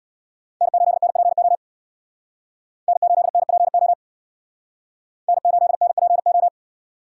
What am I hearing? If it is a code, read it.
Morse code, heard as a keyed tone, about 62 wpm, 700 Hz. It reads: N8NPO N8NPO N8NPO
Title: Speed Racing - USA 1x3 Call Signs 50wpm